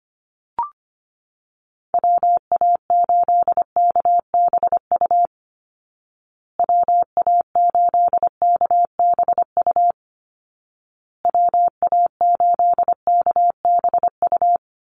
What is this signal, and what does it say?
Morse code, heard as a keyed tone, about 25 wpm, 700 Hz. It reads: WA8KBU WA8KBU WA8KBU
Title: Call Signs Repeated 2x - 25wpm